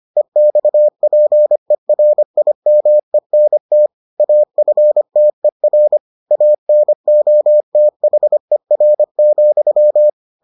Morse code, heard as a keyed tone, about 25 wpm, 600 Hz. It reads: EXPERIMENT AFTER ANOTHER,